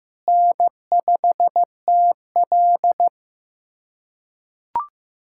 Morse code, heard as a keyed tone, about 15 wpm, 700 Hz. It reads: N5TL
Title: USA Callsign DE Callsign - 4 Characters - 15wpm